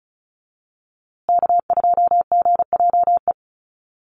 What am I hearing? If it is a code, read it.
Morse code, heard as a keyed tone, about 34 wpm, 700 Hz. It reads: K2GJE
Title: USA Call Signs Repeated 2x 35wpm